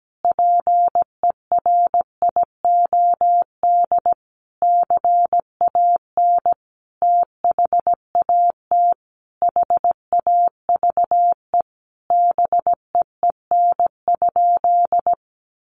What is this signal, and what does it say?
Morse code, heard as a keyed tone, about 17 wpm, 700 Hz. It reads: PERIOD CAN THAT HAVE BEEN?
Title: Sentences from Top 800 Words 17wpm